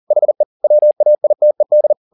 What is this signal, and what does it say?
Morse code, heard as a keyed tone, about 40 wpm, 600 Hz. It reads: HE WAITED